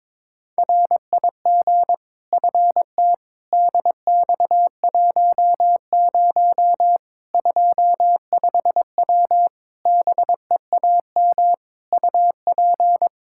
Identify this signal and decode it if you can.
Morse code, heard as a keyed tone, about 22 wpm, 700 Hz. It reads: RIG FT DX10 25W BEAM UP
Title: Practice QSO - Exchange 2 - Info - 22wpm